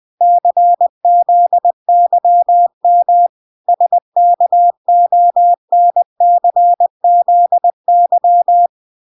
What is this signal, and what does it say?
Morse code, heard as a keyed tone, about 20 wpm, 700 Hz. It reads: CZYM SKONCZY